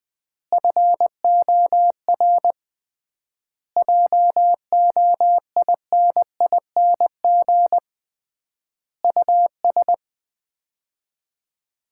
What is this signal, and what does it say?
Morse code, heard as a keyed tone, about 21 wpm, 700 Hz. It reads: FOR JOINING US
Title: 3x Word-Spacing - Sets of 5 Words - 20wpm